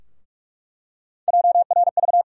Morse code, heard as a keyed tone, about 46 wpm, 700 Hz. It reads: JRV
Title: USA Call Sign Postfixes - 3 Letters 45wpm